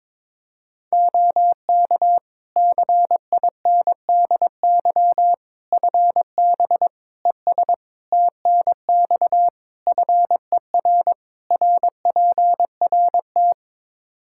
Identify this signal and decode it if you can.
Morse code, heard as a keyed tone, about 22 wpm, 700 Hz. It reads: OK CINDY FB ES TNX FER RPRT
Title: K1USN SST QSO Practice 22wpm